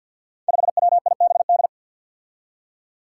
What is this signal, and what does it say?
Morse code, heard as a keyed tone, about 48 wpm, 700 Hz. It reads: 5WIBD
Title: Five Challenging Letter-Number Combinations 50wpm